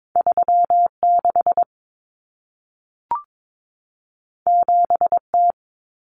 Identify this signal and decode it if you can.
Morse code, heard as a keyed tone, about 22 wpm, 700 Hz. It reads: 36 7T
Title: Common Cut Numbers - 2-Digit 22wpm